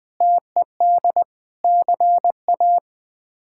Morse code, heard as a keyed tone, about 20 wpm, 700 Hz. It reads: TED CA
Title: K1USN SST QSO Practice - 20wpm